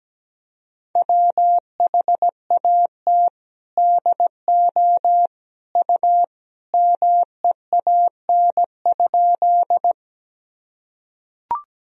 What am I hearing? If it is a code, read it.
Morse code, heard as a keyed tone, about 17 wpm, 700 Hz. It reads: WHAT DO U MEAN?
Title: Sentences from Top 200 Words and Abbreviations 17wpm